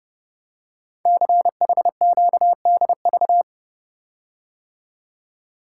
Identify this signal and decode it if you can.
Morse code, heard as a keyed tone, about 30 wpm, 700 Hz. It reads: CHQDV
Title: Code Groups - 5 Letter:Numbers - 30wpm